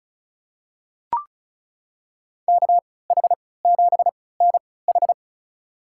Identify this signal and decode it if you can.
Morse code, heard as a keyed tone, about 36 wpm, spaced as for 20 wpm, 700 Hz. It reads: KH7NH
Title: USA Call Signs - 5 Characters - 3x Character Spacing - 35wpm